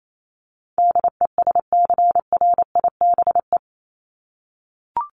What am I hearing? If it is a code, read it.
Morse code, heard as a keyed tone, about 28 wpm, 700 Hz. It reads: DESCRIBE
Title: ED Suffix - Top 100 Words with Base Words 28wpm